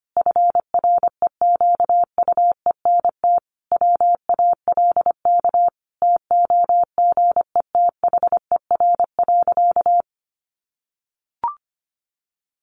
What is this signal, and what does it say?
Morse code, heard as a keyed tone, about 25 wpm, 700 Hz. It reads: FREQUENT WALK TOGETHER.
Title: Sentences from Top 1400 Words 25wpm